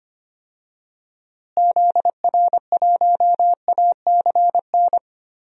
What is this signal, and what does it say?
Morse code, heard as a keyed tone, about 25 wpm, 700 Hz. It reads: ZR1ACN
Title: DXCC-100 Call Signs - 6 Characters - Repeated 3x - 25wpm